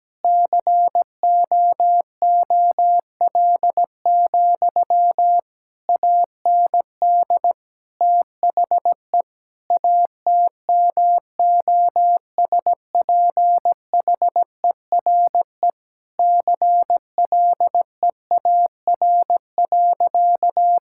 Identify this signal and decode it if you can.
Morse code, heard as a keyed tone, about 17 wpm, 700 Hz. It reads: COOL, AND THE ATMOSPHERE CLEAR.
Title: Sentences from Top 1600 Words 17wpm